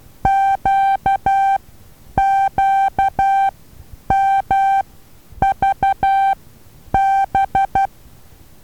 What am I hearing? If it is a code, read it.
Morse code, heard as a keyed tone, about 12 wpm, spaced as for 9 wpm, 780 Hz. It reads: QQMVB
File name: Taviro HA6VA B